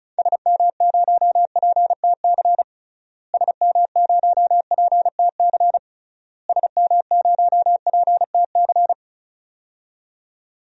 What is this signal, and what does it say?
Morse code, heard as a keyed tone, about 35 wpm, 700 Hz. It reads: SM0PTC SM0PTC SM0PTC
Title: DXCC-100 Call Signs - 6 Characters - Repeated 3x - 35wpm